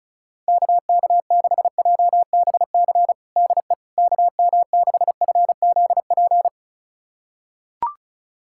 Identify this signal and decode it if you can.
Morse code, heard as a keyed tone, about 34 wpm, 700 Hz. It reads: KK6JBC DE KM6FZP
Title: USA Callsign DE Callsign - 6 Characters - 35wpm